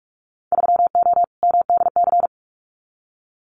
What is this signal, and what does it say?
Morse code, heard as a keyed tone, about 46 wpm, 700 Hz. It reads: 3O MDC
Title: Winter Field Day Exchanges 45wpm